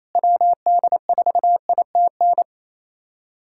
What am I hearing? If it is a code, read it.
Morse code, heard as a keyed tone, about 28 wpm, 700 Hz. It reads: WD4ITN
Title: USA Callsign DE Callsign 28wpm